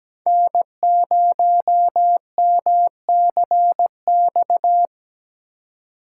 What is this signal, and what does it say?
Morse code, heard as a keyed tone, about 17 wpm, 700 Hz. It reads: N0MCX
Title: USA Call Signs - 5 Characters - 4x Character Spacing - 17wpm